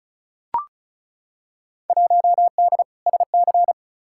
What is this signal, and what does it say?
Morse code, heard as a keyed tone, about 36 wpm, 700 Hz. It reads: 1D SC
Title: Field Day Exchanges - 35wpm